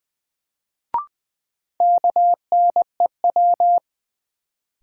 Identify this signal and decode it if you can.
Morse code, heard as a keyed tone, about 20 wpm, 700 Hz. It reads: KNEW
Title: Sentences from Top 400 Words - New Word Drill - 20wpm